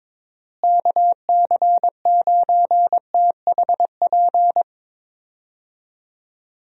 Morse code, heard as a keyed tone, about 22 wpm, 700 Hz. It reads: KC9THP
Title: USA Call Signs - 6 Characters - Repeated 2x - 22wpm